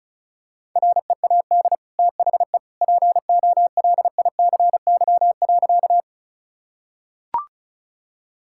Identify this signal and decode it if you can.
Morse code, heard as a keyed tone, about 34 wpm, 700 Hz. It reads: READ THE POLICY.
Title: Sentences from Top 1000 Words - Review Part 5 - 35wpm